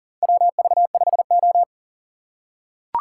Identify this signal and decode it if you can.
Morse code, heard as a keyed tone, about 40 wpm, 700 Hz. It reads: WV5O
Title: USA Call Signs - 4 Characters - 3x Character Spacing - 40wpm